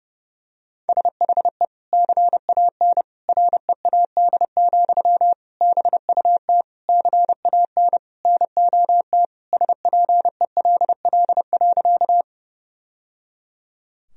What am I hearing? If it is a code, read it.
Morse code, heard as a keyed tone, about 30 wpm, 700 Hz. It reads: SHE CAN READ, BUT CAN NOT SPELL.
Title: Sentences from Top 1000 Words - Review Part 1 - 30wpm